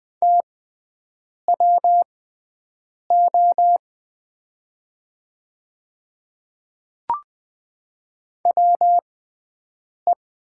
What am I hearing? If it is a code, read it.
Morse code, heard as a keyed tone, about 20 wpm, spaced as for 7 wpm, 700 Hz. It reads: TWO WE
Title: 6x Character Spacing - Top 200 Words - 20wpm